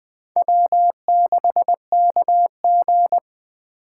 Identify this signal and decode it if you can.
Morse code, heard as a keyed tone, about 20 wpm, 700 Hz. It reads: W6KG
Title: Speed Racing - USA 1x2 Call Signs 17wpm